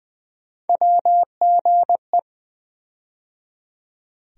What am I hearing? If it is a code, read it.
Morse code, heard as a keyed tone, about 20 wpm, 700 Hz. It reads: WGE
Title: USA Call Sign Postfixes - 3 Letters 20wpm